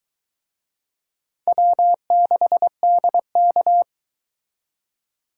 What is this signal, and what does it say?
Morse code, heard as a keyed tone, about 23 wpm, 700 Hz. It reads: W6DK